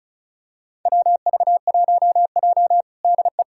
Morse code, heard as a keyed tone, about 34 wpm, 700 Hz. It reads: WV1J DE
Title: USA Callsign DE Callsign - 4 Characters - 35wpm